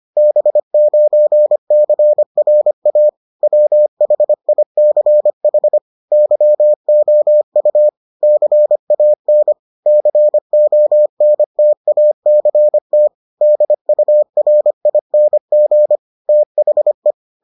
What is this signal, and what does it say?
Morse code, heard as a keyed tone, about 25 wpm, 600 Hz. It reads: B9CRA WHICH YOU CAN CONTACT DURING THE